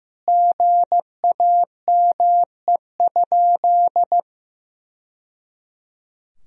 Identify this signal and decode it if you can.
Morse code, heard as a keyed tone, about 16 wpm, 700 Hz. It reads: GAME?
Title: Sentences from Top 1000 Words - Review Part 4 - 15wpm